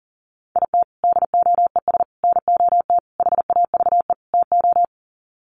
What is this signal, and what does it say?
Morse code, heard as a keyed tone, about 40 wpm, 700 Hz. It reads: IT DOES NOT HAVE TO
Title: Sets of 5 Words 40wpm